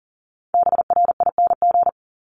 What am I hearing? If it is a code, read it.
Morse code, heard as a keyed tone, about 42 wpm, 700 Hz. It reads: BRING
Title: Multi-letter Phonemes 40wpm